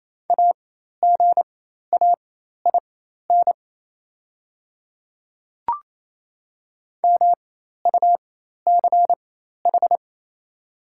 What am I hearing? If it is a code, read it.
Morse code, heard as a keyed tone, about 27 wpm, spaced as for 13 wpm, 690 Hz. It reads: AGAIN MUCH
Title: 4x Character Spacing - Top 200 Words - 28wpm